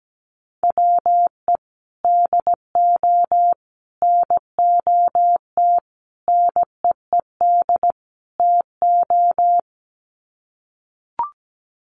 Morse code, heard as a keyed tone, about 17 wpm, 700 Hz. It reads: WE DO NOT NEED TO
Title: Sets of 5 Words Encore 17wpm